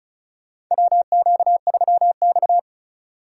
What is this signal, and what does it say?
Morse code, heard as a keyed tone, about 34 wpm, 700 Hz. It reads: WQ3X
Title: USA Call Signs - 4 Characters - 3x Character Spacing - 35wpm